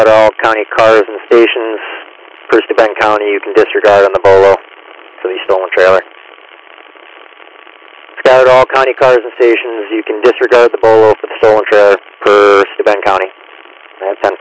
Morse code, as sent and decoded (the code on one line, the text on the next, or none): none